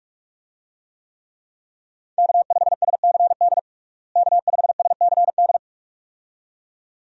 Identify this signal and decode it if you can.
Morse code, heard as a keyed tone, about 46 wpm, 700 Hz. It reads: K5SCD K5SCD
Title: USA Call Signs - 5 Characters - Repeated 2x - 45wpm